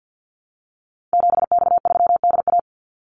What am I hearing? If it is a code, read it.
Morse code, heard as a keyed tone, about 50 wpm, 700 Hz. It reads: 7X3DA
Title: Call Signs Repeated 3x - 50wpm